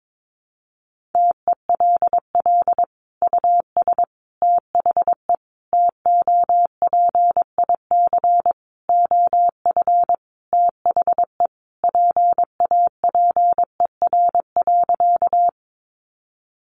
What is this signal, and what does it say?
Morse code, heard as a keyed tone, about 22 wpm, 700 Hz. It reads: TELL US THE TOPIC OF THE PAPER.
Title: Sentences from Top 1200 Words 22wpm